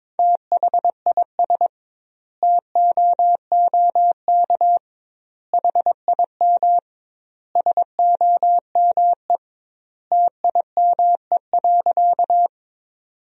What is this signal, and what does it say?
Morse code, heard as a keyed tone, about 22 wpm, 700 Hz. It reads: THIS TOOK HIM SOME TIME.
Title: 2x Word-Spacing - Sentences from Top 300 Words - 22wpm